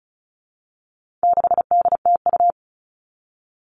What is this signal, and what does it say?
Morse code, heard as a keyed tone, about 34 wpm, 700 Hz. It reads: 6DTU
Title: All Cut Numbers - 4-Digit 35wpm